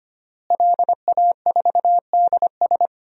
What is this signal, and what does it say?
Morse code, heard as a keyed tone, about 25 wpm, 700 Hz. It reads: LA4DS
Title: DXCC-100 Call Signs - 5 Characters - Repeated 3x - 25wpm